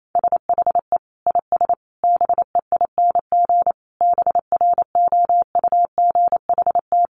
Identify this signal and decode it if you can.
Morse code, heard as a keyed tone, about 28 wpm, 700 Hz. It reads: SHE IS BEING BROUGHT
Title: Sentences from Top 600 Words 28wpm